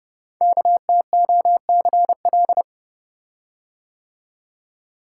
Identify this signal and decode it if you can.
Morse code, heard as a keyed tone, about 28 wpm, 700 Hz. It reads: KTOCL
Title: Code Groups - 5 Letters - 30wpm